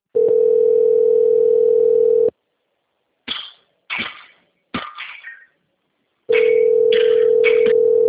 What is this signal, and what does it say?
Ringback tone (North America), 440 + 480 Hz.